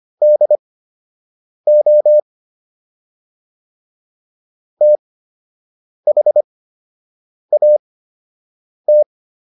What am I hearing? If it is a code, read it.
Morse code, heard as a keyed tone, about 25 wpm, spaced as for 7 wpm, 600 Hz. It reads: DO THAT